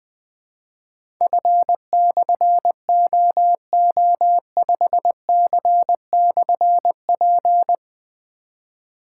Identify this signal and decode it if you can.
Morse code, heard as a keyed tone, about 20 wpm, 700 Hz. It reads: F/OO5C/P
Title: Difficult Call Signs 20wpm